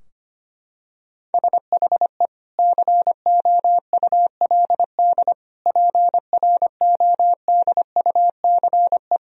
Morse code, heard as a keyed tone, about 25 wpm, 700 Hz. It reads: SHE COULD PRODUCE
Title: Sentences from Top 1000 Words - Review Part 3 - 25wpm